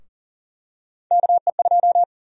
Morse code, heard as a keyed tone, about 40 wpm, 700 Hz. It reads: KE2
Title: USA Call Sign Prefixes - 40wpm